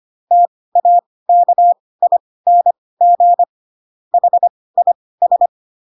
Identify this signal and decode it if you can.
Morse code, heard as a keyed tone, about 25 wpm, spaced as for 18 wpm, 700 Hz. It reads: TAKING HIS